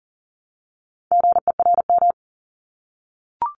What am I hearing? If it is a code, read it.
Morse code, heard as a keyed tone, about 40 wpm, 700 Hz. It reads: GERM
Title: Multi-letter Phonemes Repeated 3x 40wpm